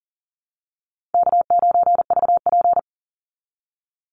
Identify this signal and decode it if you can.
Morse code, heard as a keyed tone, about 40 wpm, 700 Hz. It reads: K9VP
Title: Easy Call Signs Repeated 2x - 40wpm